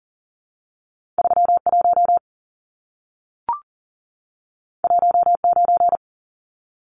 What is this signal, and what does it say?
Morse code, heard as a keyed tone, about 40 wpm, 700 Hz. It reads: 31 19
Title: Numbers - 2-Digit 40wpm